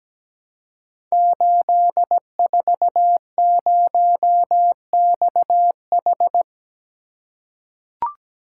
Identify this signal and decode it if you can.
Morse code, heard as a keyed tone, about 17 wpm, 700 Hz. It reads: 840XH
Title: Code Groups - 5 Letter:Numbers - 17wpm